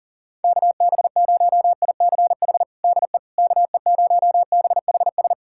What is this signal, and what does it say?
Morse code, heard as a keyed tone, about 37 wpm, 700 Hz. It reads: KB0ICH DE KE0BHS
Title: USA Callsign DE Callsign - 6 Characters - 40wpm